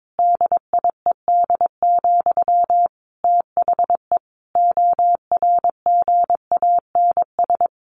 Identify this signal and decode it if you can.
Morse code, heard as a keyed tone, about 22 wpm, 700 Hz. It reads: DIED, THE ORGANS
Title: A Sound Of Thunder 22wpm